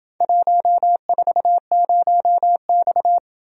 Morse code, heard as a keyed tone, about 26 wpm, 700 Hz. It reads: 140X